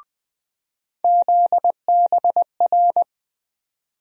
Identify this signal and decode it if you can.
Morse code, heard as a keyed tone, about 20 wpm, 700 Hz. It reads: ZBR
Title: USA Call Sign Postfixes - 3 Letters 20wpm